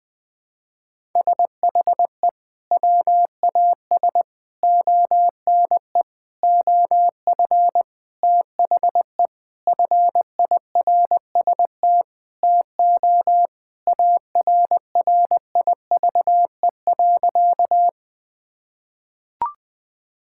Morse code, heard as a keyed tone, about 21 wpm, 700 Hz. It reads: SHE WAS ONE OF THE FIRST TO ARRIVE.
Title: Sentences with 6-Word Phrases - 20wpm